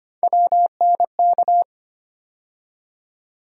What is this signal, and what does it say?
Morse code, heard as a keyed tone, about 25 wpm, 700 Hz. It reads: WNK
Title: USA Call Sign Postfixes - 3 Letters 25wpm